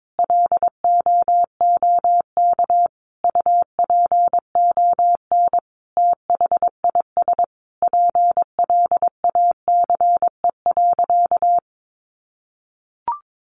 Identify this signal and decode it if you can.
Morse code, heard as a keyed tone, about 22 wpm, 700 Hz. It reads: LOOK UPON THIS PLACE.
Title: Sentences from Top 1100 Words 22wpm